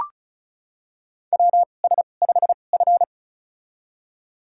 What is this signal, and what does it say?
Morse code, heard as a keyed tone, about 36 wpm, spaced as for 25 wpm, 690 Hz. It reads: WS5F
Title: USA Call Signs - 4 Characters - 2x Character Spacing - 35wpm